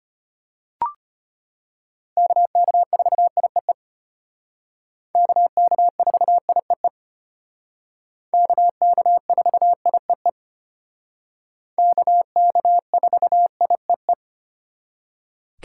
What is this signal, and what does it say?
Morse code, heard as a keyed tone, about 32 wpm, 700 Hz. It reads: KK4IEE KK4IEE KK4IEE KK4IEE